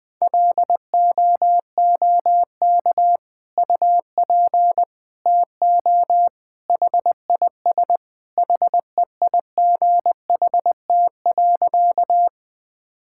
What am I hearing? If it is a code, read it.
Morse code, heard as a keyed tone, about 20 wpm, 700 Hz. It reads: LOOK UP TO HIS HEIGHT.
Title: Sentences from Top 1600 Words 20wpm